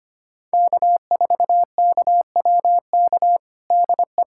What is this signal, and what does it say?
Morse code, heard as a keyed tone, about 25 wpm, 700 Hz. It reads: K4KWK DE
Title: USA Callsign DE Callsign - 5 Characters - 25wpm